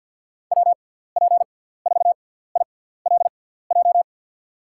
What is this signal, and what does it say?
Morse code, heard as a keyed tone, about 53 wpm, 700 Hz. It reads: W P 4 I L J
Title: USA Call Signs - 6x Character Spacing - 50wpm